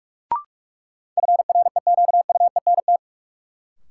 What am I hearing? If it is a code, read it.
Morse code, heard as a keyed tone, about 44 wpm, 700 Hz. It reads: FREQUENT
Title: Sentences from Top 1400 Words - New Word Drill Alt - 45wpm